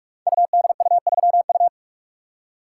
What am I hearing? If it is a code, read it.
Morse code, heard as a keyed tone, about 46 wpm, 700 Hz. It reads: UDU3U
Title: Code Groups - 5 Letter:Numbers - 45wpm